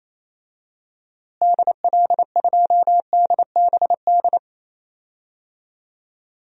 Morse code, heard as a keyed tone, about 28 wpm, 700 Hz. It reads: DL2DBD
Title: DXCC-100 Call Signs - 6 Characters - Repeated 2x - 28wpm